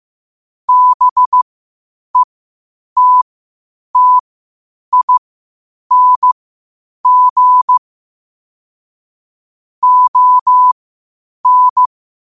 Morse code, heard as a keyed tone, about 15 wpm, spaced as for 8 wpm, 1000 Hz. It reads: BETTING ON
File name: qotdfiles / 8wpm